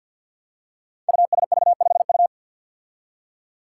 Morse code, heard as a keyed tone, about 50 wpm, 700 Hz. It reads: USV5U
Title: Five Challenging Letter-Number Combinations 50wpm